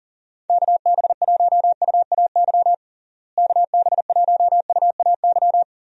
Morse code, heard as a keyed tone, about 40 wpm, 700 Hz. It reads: KB1UAY KB1UAY
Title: USA Call Signs - 6 Characters - Repeated 2x - 40wpm